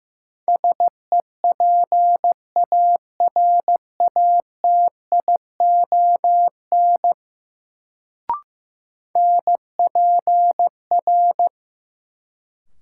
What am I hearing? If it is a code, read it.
Morse code, heard as a keyed tone, about 15 wpm, 700 Hz. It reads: SEPARATION NPR
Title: Sets of 5 Words - New Word Drill - 15wpm